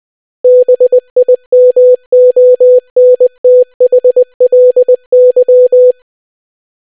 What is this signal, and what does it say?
Morse code, heard as a keyed tone, about 20 wpm, 500 Hz. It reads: BIMONTHLY